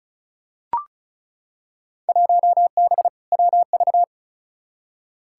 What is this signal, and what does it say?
Morse code, heard as a keyed tone, about 36 wpm, 700 Hz. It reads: 1B WV
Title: Field Day Exchanges - 35wpm